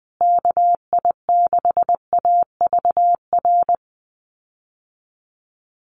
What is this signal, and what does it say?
Morse code, heard as a keyed tone, about 20 wpm, 700 Hz. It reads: KI6AVR
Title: USA Call Signs Repeated 2x 20wpm